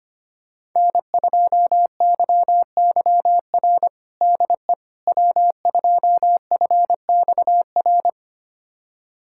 Morse code, heard as a keyed tone, about 25 wpm, 700 Hz. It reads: N2YYR DE W2FXR
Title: USA Callsign DE Callsign - 5 Characters - 25wpm